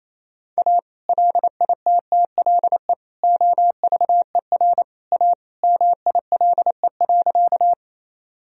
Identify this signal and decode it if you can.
Morse code, heard as a keyed tone, about 28 wpm, 700 Hz. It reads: A LITTLE OVER A MILE.
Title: Sentences from Top 1000 Words - Review Part 1 - 28wpm